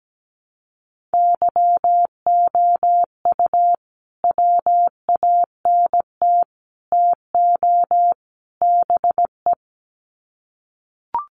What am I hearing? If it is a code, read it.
Morse code, heard as a keyed tone, about 17 wpm, 700 Hz. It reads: YOU WANT TO BE
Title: Sets of 4 Words 17wpm